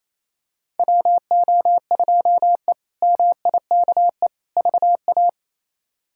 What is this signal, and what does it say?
Morse code, heard as a keyed tone, about 28 wpm, 700 Hz. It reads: WO2E MIKE VA
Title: K1USN SST QSO Practice - 28wpm